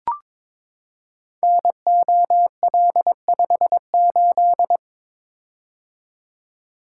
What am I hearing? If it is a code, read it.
Morse code, heard as a keyed tone, about 22 wpm, 700 Hz. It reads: NOL58
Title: Code Groups - 5 Letter:Numbers - 22wpm